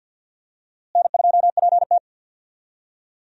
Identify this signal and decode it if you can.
Morse code, heard as a keyed tone, about 50 wpm, 700 Hz. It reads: N2PT
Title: USA Call Signs - 4 Characters - Repeated 2x - 50wpm